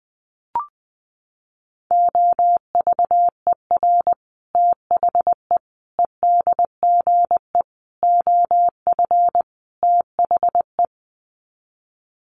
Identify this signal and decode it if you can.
Morse code, heard as a keyed tone, about 20 wpm, 700 Hz. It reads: OVER THE EDGE OF THE
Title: Sets of 5 Words Encore2 20wpm